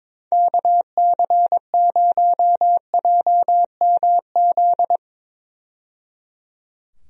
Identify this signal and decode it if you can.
Morse code, heard as a keyed tone, about 22 wpm, 700 Hz. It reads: KC0JMZ
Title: USA Call Signs - 6 Characters - 22wpm